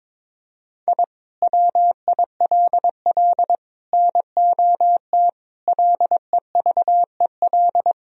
Morse code, heard as a keyed tone, about 22 wpm, 700 Hz. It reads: I WILL NOT LEVEL
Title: Sentences from Top 1000 Words - Review Part 4 - 22wpm